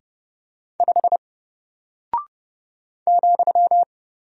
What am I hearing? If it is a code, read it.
Morse code, heard as a keyed tone, about 30 wpm, 700 Hz. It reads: 5 ,
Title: Single Character (Letter:Number:Punctuation:Prosign) Alt 30wpm